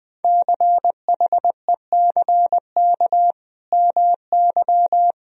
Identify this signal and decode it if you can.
Morse code, heard as a keyed tone, about 20 wpm, 700 Hz. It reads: CHECK MY
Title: Sentences from Top 1000 Words - Review Part 3 - 20wpm